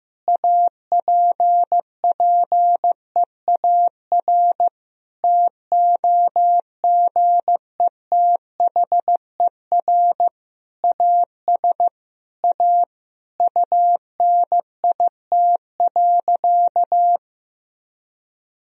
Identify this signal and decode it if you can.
Morse code, heard as a keyed tone, about 15 wpm, 700 Hz. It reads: APPEAR TOGETHER AS A UNIT.
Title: Sentences from Top 1000 Words - Review Part 2 - 15wpm